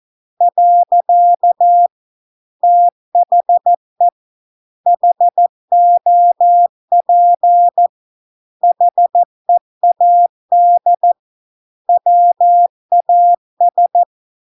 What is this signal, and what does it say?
Morse code, heard as a keyed tone, about 14 wpm, 700 Hz. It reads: . THE HOP HEAD WAS